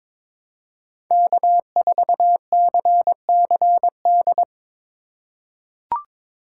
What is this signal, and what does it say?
Morse code, heard as a keyed tone, about 22 wpm, 700 Hz. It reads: K4CCD
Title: USA Call Signs - 5 Characters - 4x Character Spacing - 22wpm